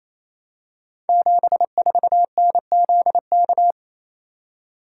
Morse code, heard as a keyed tone, about 28 wpm, 700 Hz. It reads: 74NZK
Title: Code Groups - 5 Letter:Numbers - 28wpm